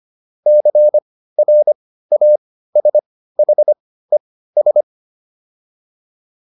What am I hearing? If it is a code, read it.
Morse code, heard as a keyed tone, about 25 wpm, spaced as for 15 wpm, 600 Hz. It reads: CRASHES